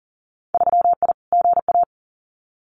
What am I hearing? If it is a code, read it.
Morse code, heard as a keyed tone, about 40 wpm, 700 Hz. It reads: 3I GA
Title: Winter Field Day Exchanges 40wpm